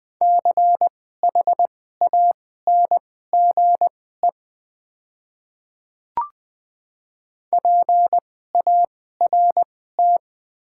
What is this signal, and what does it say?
Morse code, heard as a keyed tone, about 20 wpm, spaced as for 14 wpm, 700 Hz. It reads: CHANGE PART
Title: 2x Character Spacing - Top 300 Words - 20wpm